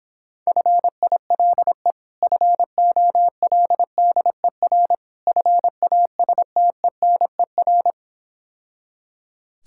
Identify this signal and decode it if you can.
Morse code, heard as a keyed tone, about 26 wpm, 700 Hz. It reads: FILE FOLDER FASTENER